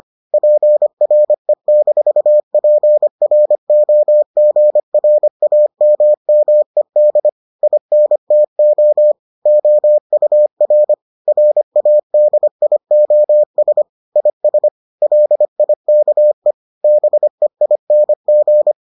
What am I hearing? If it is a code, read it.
Morse code, heard as a keyed tone, about 25 wpm, 600 Hz. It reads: PRE-PROGRAMMED INTO OUR RADIOS IS LIKE BEING